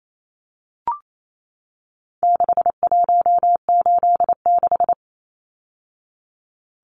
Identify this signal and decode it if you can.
Morse code, heard as a keyed tone, about 28 wpm, 700 Hz. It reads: 6186
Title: Numbers - 4-Digit 28wpm